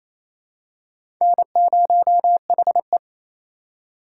Morse code, heard as a keyed tone, about 27 wpm, 700 Hz. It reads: N0HE
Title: USA Call Signs - 2x Character Spacing - 28wpm